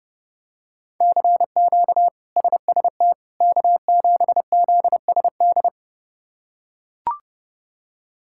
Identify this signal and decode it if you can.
Morse code, heard as a keyed tone, about 30 wpm, 700 Hz. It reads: CQ SST K7ZSD
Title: K1USN SST QSO Practice - 30wpm